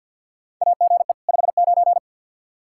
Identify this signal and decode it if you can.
Morse code, heard as a keyed tone, about 47 wpm, 700 Hz. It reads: AGE 59
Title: Practice QSO - Exchange 3 - More Info - 50wpm